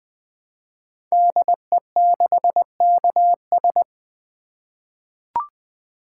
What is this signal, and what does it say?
Morse code, heard as a keyed tone, about 20 wpm, 700 Hz. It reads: DE6KS
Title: Code Groups - 5 Letter:Numbers - 20wpm